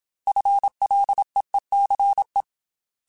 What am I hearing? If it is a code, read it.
Morse code, heard as a keyed tone, about 27 wpm, 800 Hz. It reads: FLEECE
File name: FL / FL mp3s / FL Prefixes 3x2 @25